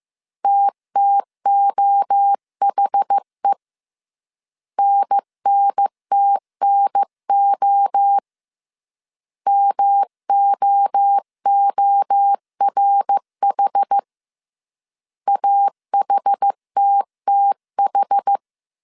Morse code, heard as a keyed tone, about 15 wpm, 800 Hz. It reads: TTOHE NNTNO MOORH AHTTH
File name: CW / DET / 1 3 1500C S A 14wpm 2447